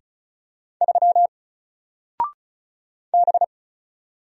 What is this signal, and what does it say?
Morse code, heard as a keyed tone, about 33 wpm, 700 Hz. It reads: 3 B
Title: Single Letter-Number 35wpm